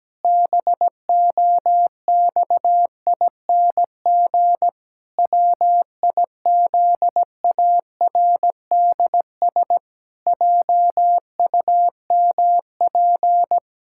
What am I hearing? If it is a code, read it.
Morse code, heard as a keyed tone, about 17 wpm, 700 Hz. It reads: BOXING WIZARDS JUMP
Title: Pangram Challenge v2 - 17wpm